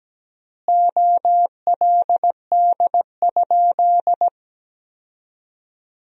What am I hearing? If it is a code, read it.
Morse code, heard as a keyed tone, about 17 wpm, 700 Hz. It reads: OLD?
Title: 4x Word-Spacing - Sentences from Top 400 Words - 17wpm